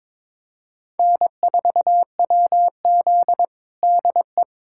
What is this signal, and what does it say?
Morse code, heard as a keyed tone, about 22 wpm, 700 Hz. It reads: N4WZ DE
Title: USA Callsign DE Callsign - 4 Characters - 22wpm